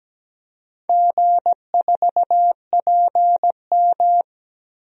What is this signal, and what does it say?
Morse code, heard as a keyed tone, about 17 wpm, 700 Hz. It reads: G4PM
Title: DXCC-100 Call Signs - 4 Characters - 17wpm